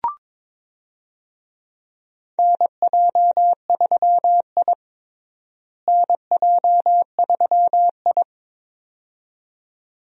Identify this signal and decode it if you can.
Morse code, heard as a keyed tone, about 22 wpm, 700 Hz. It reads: NJ3I NJ3I
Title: USA Call Signs - 4 Characters - Repeated 2x - 22wpm